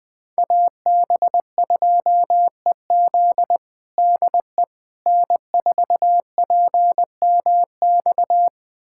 Morse code, heard as a keyed tone, about 20 wpm, 700 Hz. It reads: AB2EZ DE N4PMX
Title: USA Callsign DE Callsign - 5 Characters - 20wpm